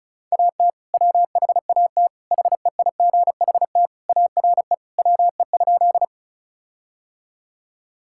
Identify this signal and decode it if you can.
Morse code, heard as a keyed tone, about 34 wpm, 700 Hz. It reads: AT WHAT HEIGHT ARE WE?
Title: Sentences from Top 1600 Words 35wpm